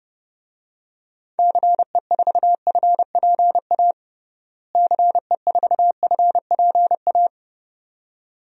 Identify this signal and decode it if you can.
Morse code, heard as a keyed tone, about 31 wpm, 700 Hz. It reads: CE4FPA CE4FPA
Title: DXCC-100 Call Signs - 6 Characters - Repeated 2x - 30wpm